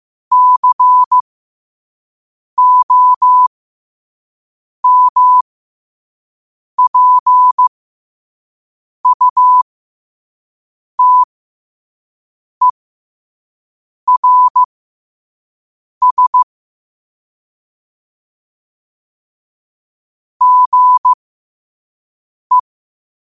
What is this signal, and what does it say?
Morse code, heard as a keyed tone, about 15 wpm, spaced as for 5 wpm, 1000 Hz. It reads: COMPUTERS GE
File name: qotdfiles / 5wpm